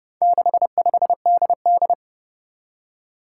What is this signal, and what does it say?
Morse code, heard as a keyed tone, about 30 wpm, 700 Hz. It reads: 65DD
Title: Four Challenging Letter-Number Combinations 30wpm